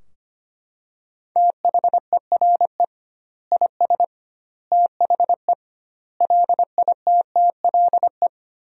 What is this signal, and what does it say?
Morse code, heard as a keyed tone, about 25 wpm, 700 Hz. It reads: THERE IS THE LITTLE
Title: 2x Word-Spacing - Sentences from Top 200 Words - 25wpm